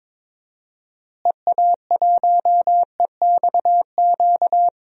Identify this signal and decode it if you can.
Morse code, heard as a keyed tone, about 22 wpm, 700 Hz. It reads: EA1EXQ
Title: DXCC-100 Call Signs - 6 Characters - Repeated 2x - 22wpm